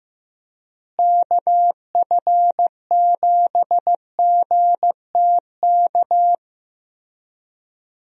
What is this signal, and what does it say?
Morse code, heard as a keyed tone, about 15 wpm, 700 Hz. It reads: KF7GTK
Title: USA Call Signs - 6 Characters - Repeated 2x - 15wpm